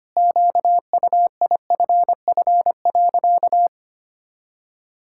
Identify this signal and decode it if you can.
Morse code, heard as a keyed tone, about 25 wpm, 700 Hz. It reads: QUIFF.
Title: Pangram Challenge v2 - 25wpm